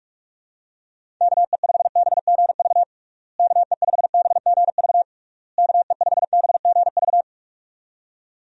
Morse code, heard as a keyed tone, about 46 wpm, 700 Hz. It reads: KE5BGV KE5BGV KE5BGV
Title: USA Call Signs - 6 Characters - Repeated 3x - 45wpm